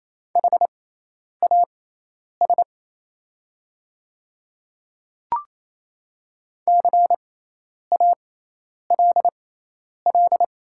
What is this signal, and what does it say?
Morse code, heard as a keyed tone, about 27 wpm, spaced as for 10 wpm, 700 Hz. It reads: HAS CALL
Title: 6x Character Spacing - Top 100 Words - 28wpm